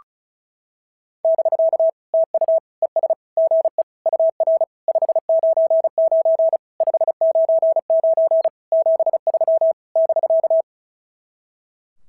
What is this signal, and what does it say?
Morse code, heard as a keyed tone, about 33 wpm, 650 Hz. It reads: <BK> TU ES GE UR 599 599 73 <BK>
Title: SOTA QSO Practice 35wpm